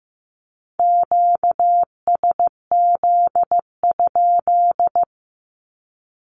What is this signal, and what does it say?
Morse code, heard as a keyed tone, about 15 wpm, 700 Hz. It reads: QSZ?
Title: Q-Codes Repeated 3x 15wpm